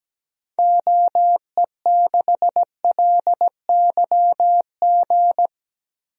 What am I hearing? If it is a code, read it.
Morse code, heard as a keyed tone, about 17 wpm, 700 Hz. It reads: OE6LYG
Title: DXCC-100 Call Signs - 6 Characters - Repeated 2x - 17wpm